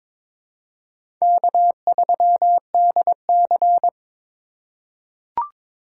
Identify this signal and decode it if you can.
Morse code, heard as a keyed tone, about 22 wpm, 700 Hz. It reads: K3DC
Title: USA Call Signs - 3x Character Spacing - 22wpm